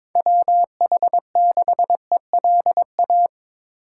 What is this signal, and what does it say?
Morse code, heard as a keyed tone, about 22 wpm, 700 Hz. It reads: WH6ELA
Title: USA Call Signs - 6 Characters - 3x Character Spacing - 22wpm